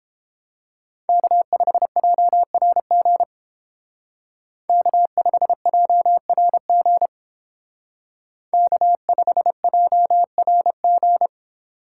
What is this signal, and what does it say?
Morse code, heard as a keyed tone, about 30 wpm, 700 Hz. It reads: K5JRG K5JRG K5JRG